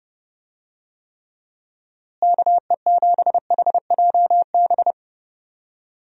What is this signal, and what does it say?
Morse code, heard as a keyed tone, about 30 wpm, 700 Hz. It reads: KE7HJB
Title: USA Call Signs - 6 Characters - Repeated 2x - 30wpm